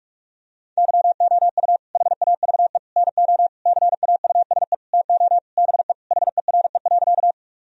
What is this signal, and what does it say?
Morse code, heard as a keyed tone, about 46 wpm, 700 Hz. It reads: YOU HAVE NO CAUSE TO BE HERE.
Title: Sentences from Top 1000 Words - Review Part 1 - 45wpm